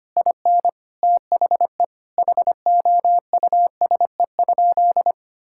Morse code, heard as a keyed tone, about 25 wpm, 700 Hz. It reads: IN THE HOUSE?
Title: Sentences from Top 1000 Words - Review Part 5 - 25wpm